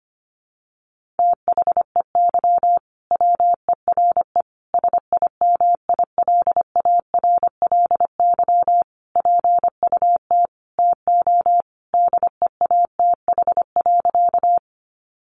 Morse code, heard as a keyed tone, about 25 wpm, 700 Hz. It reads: THEY WERE SIMILARLY PUT TO DEATH.
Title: Sentences from Top 1400 Words 25wpm